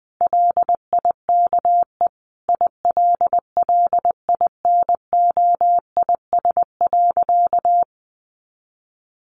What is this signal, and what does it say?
Morse code, heard as a keyed tone, about 20 wpm, 700 Hz. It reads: LIKE ILLINOIS.
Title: A Sound Of Thunder 20wpm